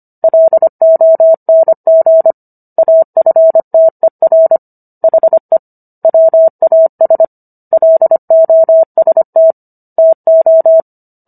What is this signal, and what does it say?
Morse code, heard as a keyed tone, about 25 wpm, 650 Hz. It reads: LONG AFTER HE WAS LOST TO